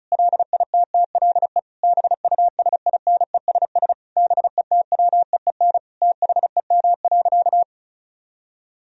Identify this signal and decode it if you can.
Morse code, heard as a keyed tone, about 34 wpm, 700 Hz. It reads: LITTLE BUSINESS BETWEEN THEM.
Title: Sentences from Top 1000 Words - Review Part 1 - 35wpm